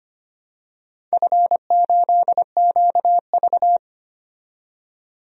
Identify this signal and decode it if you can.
Morse code, heard as a keyed tone, about 25 wpm, 700 Hz. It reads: F8QV
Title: DXCC-100 Call Signs - 4 Characters - 25wpm